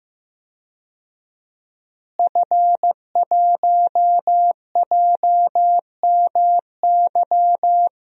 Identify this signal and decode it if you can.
Morse code, heard as a keyed tone, about 15 wpm, 700 Hz. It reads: F1JMY
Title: DXCC-100 Call Signs - 5 Characters - Repeated 2x - 15wpm